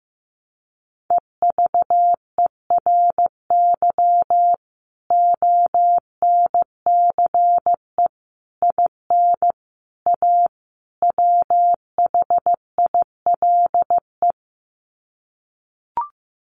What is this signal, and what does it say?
Morse code, heard as a keyed tone, about 15 wpm, 700 Hz. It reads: EVERY ONCE IN A WHILE
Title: Sets of 5 Words 15wpm